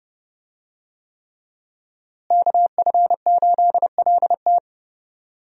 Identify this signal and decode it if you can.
Morse code, heard as a keyed tone, about 29 wpm, 700 Hz. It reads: KF8LT
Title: USA Call Signs - 5 Characters - Repeated 3x - 30wpm